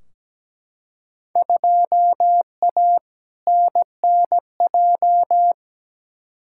Morse code, heard as a keyed tone, about 17 wpm, 700 Hz. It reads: 2A NNJ
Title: Field Day Exchanges - 17wpm